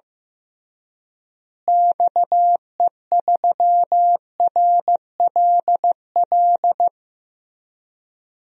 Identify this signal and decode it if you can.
Morse code, heard as a keyed tone, about 15 wpm, 700 Hz. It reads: XE3RLL
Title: DXCC-100 Call Signs - 6 Characters - Repeated 3x - 15wpm